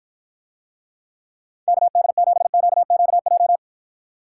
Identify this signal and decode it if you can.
Morse code, heard as a keyed tone, about 53 wpm, 700 Hz. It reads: KD7QQJ